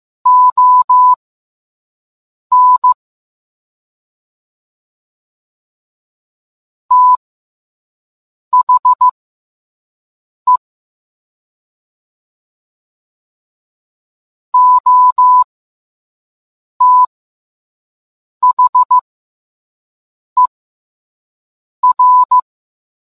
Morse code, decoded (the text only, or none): ON THE OTHER